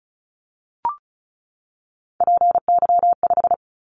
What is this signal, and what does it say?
Morse code, heard as a keyed tone, about 36 wpm, 700 Hz. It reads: PY5
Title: Callsign Prefixes for Top 100 DXCC - 35wpm